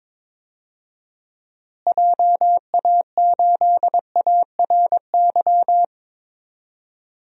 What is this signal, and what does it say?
Morse code, heard as a keyed tone, about 22 wpm, 700 Hz. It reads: JA8ARY
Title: DXCC-100 Call Signs - 6 Characters - Repeated 2x - 22wpm